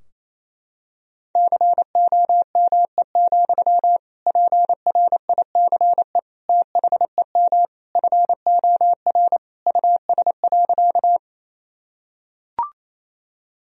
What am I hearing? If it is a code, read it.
Morse code, heard as a keyed tone, about 28 wpm, 700 Hz. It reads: COME, PRICE THEM FOR US.
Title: Sentences from Top 1000 Words - Review Part 4 - 28wpm